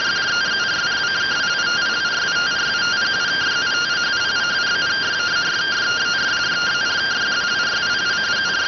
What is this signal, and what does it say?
Two-tone teleprinter signal, about 170 Hz shift.